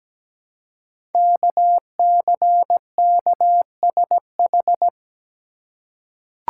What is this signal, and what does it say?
Morse code, heard as a keyed tone, about 17 wpm, 700 Hz. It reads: KCKSH
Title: Code Groups - 5 Letters - 17wpm